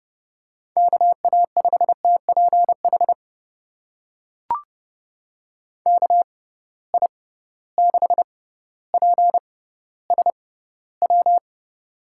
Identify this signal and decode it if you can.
Morse code, heard as a keyed tone, about 31 wpm, spaced as for 16 wpm, 700 Hz. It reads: KA5TPH KI6PSW
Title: USA Call Signs - 6 Characters - 6x Character Spacing - 30wpm